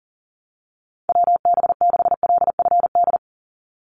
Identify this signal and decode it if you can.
Morse code, heard as a keyed tone, about 40 wpm, 700 Hz. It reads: WB6LFD
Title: USA Call Signs 40wpm